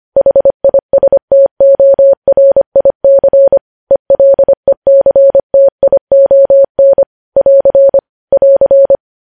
Morse code, heard as a keyed tone, about 25 wpm, 550 Hz. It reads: HISTORIC ELECTION <AR> <AR>
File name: podcast / News headlines-1x-25x25